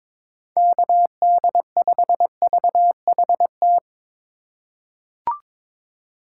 Morse code, heard as a keyed tone, about 22 wpm, 700 Hz. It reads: KD5VHT
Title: USA Call Signs - 6 Characters - 4x Character Spacing - 22wpm